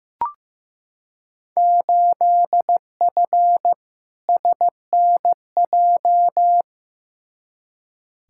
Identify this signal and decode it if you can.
Morse code, heard as a keyed tone, about 15 wpm, 700 Hz. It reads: 8F SNJ
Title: Field Day Exchanges Encore - 15wpm